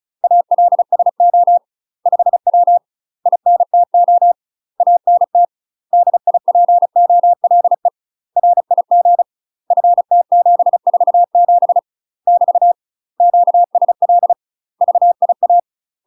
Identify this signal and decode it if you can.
Morse code, heard as a keyed tone, about 34 wpm, 700 Hz. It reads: ALSO 5W INTO ANT DIPOLE RIG FT747 <BT> QSL VIA
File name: UK MASTER QSOs TRAINING / 35WPM / UK MASTER QSOs 0001